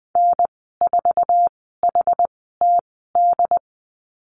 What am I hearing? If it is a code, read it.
Morse code, heard as a keyed tone, about 20 wpm, spaced as for 14 wpm, 700 Hz. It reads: N4HTD
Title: 069 Lesson 12g - Letter d - Call Signs 20wpm